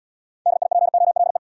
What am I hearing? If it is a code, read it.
Morse code, heard as a keyed tone, about 64 wpm, 700 Hz. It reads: DIPOLE